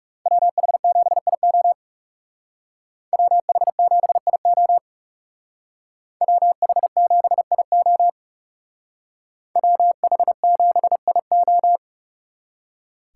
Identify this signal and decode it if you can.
Morse code, heard as a keyed tone, about 38 wpm, 700 Hz. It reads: WH7IO WH7IO WH7IO WH7IO